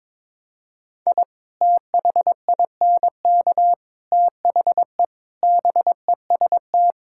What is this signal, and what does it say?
Morse code, heard as a keyed tone, about 22 wpm, 700 Hz. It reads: I THINK THE BEST
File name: practice / 22wpm / Sets of 8 Words - 22wpm